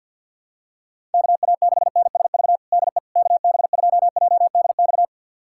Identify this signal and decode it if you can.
Morse code, heard as a keyed tone, about 50 wpm, 700 Hz. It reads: KA6NSV DE KB2JDX
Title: USA Callsign DE Callsign 50wpm